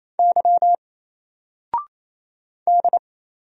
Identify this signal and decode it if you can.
Morse code, heard as a keyed tone, about 27 wpm, 700 Hz. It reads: Y D
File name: Single-Letter-Number-28wpm